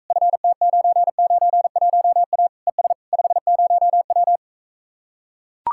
Morse code, heard as a keyed tone, about 45 wpm, 700 Hz. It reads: FT991A ES 50W